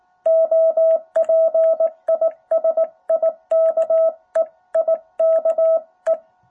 Morse code, heard as a keyed tone, about 18 wpm, 660 Hz. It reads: OP IS IKE IKE
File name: jh8cbh / 2021cw